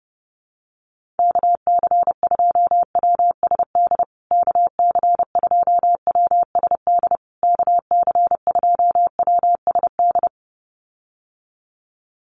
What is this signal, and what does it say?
Morse code, heard as a keyed tone, about 30 wpm, 700 Hz. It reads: KC2WSD KC2WSD KC2WSD
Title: USA Call Signs Repeated 3x 30wpm